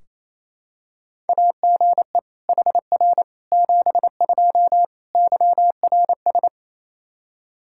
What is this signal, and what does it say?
Morse code, heard as a keyed tone, about 28 wpm, 700 Hz. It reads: AGE HR 72 YRS
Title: K1USN SST QSO Practice 28wpm